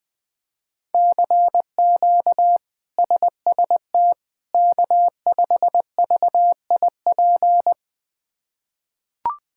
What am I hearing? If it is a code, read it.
Morse code, heard as a keyed tone, about 20 wpm, 700 Hz. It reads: CQ SST K5VIP
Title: K1USN SST QSO Practice - 20wpm